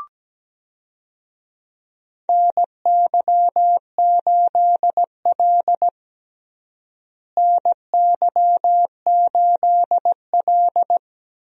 Morse code, heard as a keyed tone, about 17 wpm, 700 Hz. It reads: NY8L NY8L
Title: USA Call Signs - 4 Characters - Repeated 2x - 17wpm